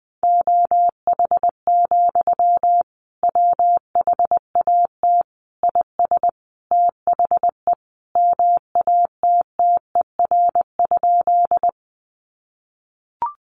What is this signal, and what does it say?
Morse code, heard as a keyed tone, about 20 wpm, 700 Hz. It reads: OH, WHAT IS THE MATTER?
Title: Sentences from Top 700 Words 20wpm